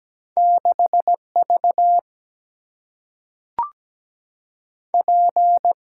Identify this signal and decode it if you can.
Morse code, heard as a keyed tone, about 17 wpm, 700 Hz. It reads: 6V P
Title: Single-Letter Single-Number 17wpm